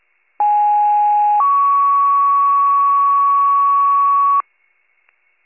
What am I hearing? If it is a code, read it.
Two-tone page, heard as a one-second tone, then a three-second tone, about 830 then 1120 Hz. It is heard from 0.4 to 4.4 s.